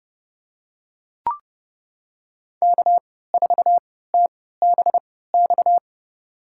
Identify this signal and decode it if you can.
Morse code, heard as a keyed tone, about 30 wpm, spaced as for 17 wpm, 700 Hz. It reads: K4TBX
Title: USA Call Signs - 5 Characters - 3x Character Spacing - 30wpm